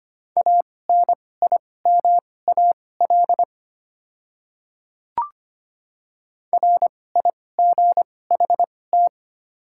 Morse code, heard as a keyed tone, about 25 wpm, spaced as for 18 wpm, 700 Hz. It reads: ANIMAL RIGHT
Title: 2x Character Spacing - Top 200 Words - 25wpm